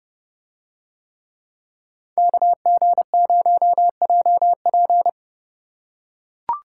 Morse code, heard as a keyed tone, about 30 wpm, 700 Hz. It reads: KG0JP